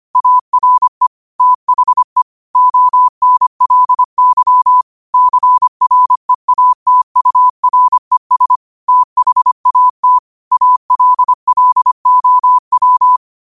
Morse code, heard as a keyed tone, about 25 wpm, 1000 Hz. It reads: ARE THE ONLY CREATURES THAT ALLOW
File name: qotdfiles / 25wpm